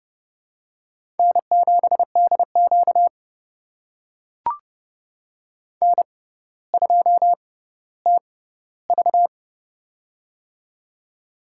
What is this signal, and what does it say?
Morse code, heard as a keyed tone, about 30 wpm, spaced as for 15 wpm, 700 Hz. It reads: N7DQ N2TV
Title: USA Call Signs - 4 Characters - 6x Character Spacing - 30wpm